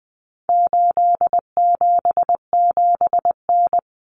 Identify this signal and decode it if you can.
Morse code, heard as a keyed tone, about 20 wpm, 700 Hz. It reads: 877N
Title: Common Cut Numbers - 4-Digit 20wpm